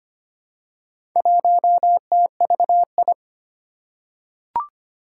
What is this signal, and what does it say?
Morse code, heard as a keyed tone, about 25 wpm, 700 Hz. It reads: 1TVI
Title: Four Challenging Letter-Number Combinations 25wpm